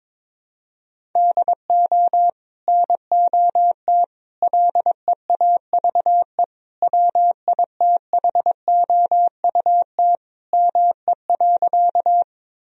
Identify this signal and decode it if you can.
Morse code, heard as a keyed tone, about 22 wpm, 700 Hz. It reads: DO NOT LEAVE WITHOUT ME.
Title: Sentences from Top 1000 Words - Review Part 3 - 22wpm